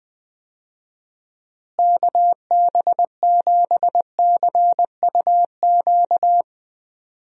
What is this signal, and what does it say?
Morse code, heard as a keyed tone, about 20 wpm, 700 Hz. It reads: KB7CUQ
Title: USA Call Signs - 6 Characters - Repeated 3x - 20wpm